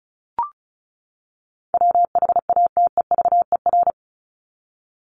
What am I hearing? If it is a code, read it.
Morse code, heard as a keyed tone, about 34 wpm, 700 Hz. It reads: WHATEVER
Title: Top 1001-1250 Words 35wpm